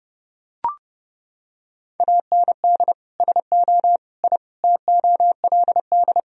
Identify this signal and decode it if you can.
Morse code, heard as a keyed tone, about 30 wpm, 700 Hz. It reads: AND SO I TOLD
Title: Sentences from Top 1000 Words - Review Part 4 - 30wpm